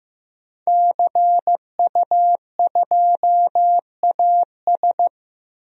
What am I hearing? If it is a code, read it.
Morse code, heard as a keyed tone, about 15 wpm, 700 Hz. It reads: CU2AS
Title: DXCC-100 Call Signs - 5 Characters - 15wpm